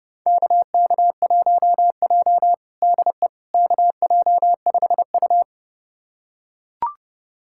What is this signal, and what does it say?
Morse code, heard as a keyed tone, about 30 wpm, 700 Hz. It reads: KK1J DE KJ5U
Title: USA Callsign DE Callsign - 4 Characters - 30wpm